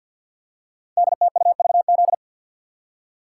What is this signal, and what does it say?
Morse code, heard as a keyed tone, about 51 wpm, 700 Hz. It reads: DTUVZ
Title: Code Groups - 5 Letters - 50wpm